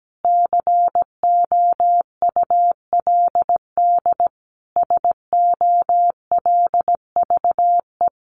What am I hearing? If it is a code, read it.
Morse code, heard as a keyed tone, about 17 wpm, 700 Hz. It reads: COULD SOLVE
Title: Sentences from Top 1000 Words 17wpm